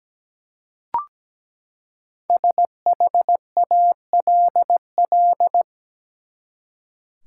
Morse code, heard as a keyed tone, about 17 wpm, 700 Hz. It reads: SHALL
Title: Five Letter Words - 17wpm